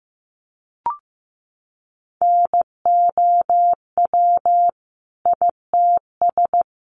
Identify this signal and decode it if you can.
Morse code, heard as a keyed tone, about 15 wpm, 700 Hz. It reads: NOW ITS
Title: Sentences from Top 600 Words 15wpm